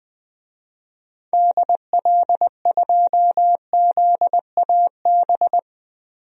Morse code, heard as a keyed tone, about 20 wpm, 700 Hz. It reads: DL2ZAB
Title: DXCC-100 Call Signs - 6 Characters - Repeated 3x - 20wpm